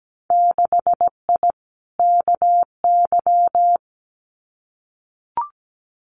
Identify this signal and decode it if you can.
Morse code, heard as a keyed tone, about 17 wpm, 700 Hz. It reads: 6I KY
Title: Winter Field Day Exchanges Encore 17wpm